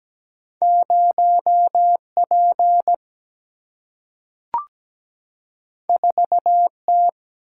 Morse code, heard as a keyed tone, about 17 wpm, 700 Hz. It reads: 0 P 4 T
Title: Single-Letter Single-Number 17wpm